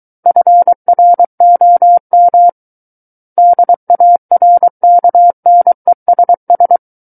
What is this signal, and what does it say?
Morse code, heard as a keyed tone, about 23 wpm, 700 Hz. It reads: FROM DARKNESS
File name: Book One / 14-Chapter 14 - 23 WPM